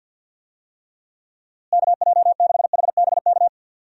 Morse code, heard as a keyed tone, about 51 wpm, 700 Hz. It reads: KJ6HBK
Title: USA Call Signs - 6 Characters - Repeated 3x - 50wpm